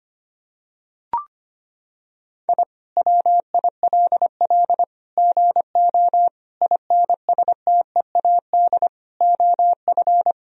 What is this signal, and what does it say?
Morse code, heard as a keyed tone, about 25 wpm, 700 Hz. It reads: I WILL GO INSTEAD OF
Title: Sentences from Top 1000 Words - Review Part 4 - 25wpm